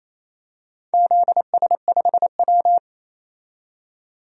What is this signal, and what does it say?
Morse code, heard as a keyed tone, about 28 wpm, 700 Hz. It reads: ZS5W
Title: DXCC-100 Call Signs - 4 Characters - 28wpm